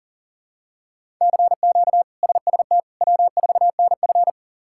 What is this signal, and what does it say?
Morse code, heard as a keyed tone, about 40 wpm, 700 Hz. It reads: CQ SST W4NF
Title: K1USN SST QSO Practice - 40wpm